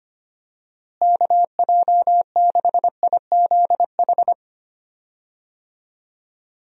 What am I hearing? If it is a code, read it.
Morse code, heard as a keyed tone, about 25 wpm, 700 Hz. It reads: KJ6IZH
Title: USA Call Signs - 6 Characters - Repeated 2x - 25wpm